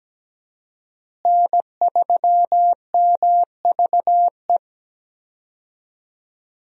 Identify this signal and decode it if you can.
Morse code, heard as a keyed tone, about 17 wpm, 700 Hz. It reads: N3MVE
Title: USA Call Signs - 5 Characters - Repeated 3x - 17wpm